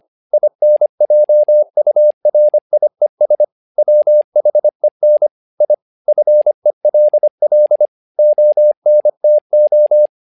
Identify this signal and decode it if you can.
Morse code, heard as a keyed tone, about 25 wpm, 600 Hz. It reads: INJURIES WHEN I FELL ONTO
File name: learning center / tower trouble / tower trouble 25wpm